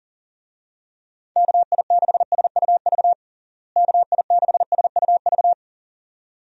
Morse code, heard as a keyed tone, about 40 wpm, 700 Hz. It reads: KI6SUV KI6SUV
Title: USA Call Signs - 6 Characters - Repeated 2x - 40wpm